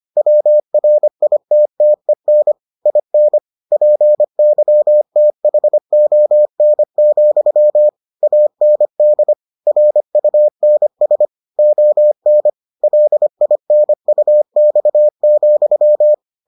Morse code, heard as a keyed tone, about 25 wpm, 600 Hz. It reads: WRITTEN IN PYTHON, AND RUNS ON LINUX,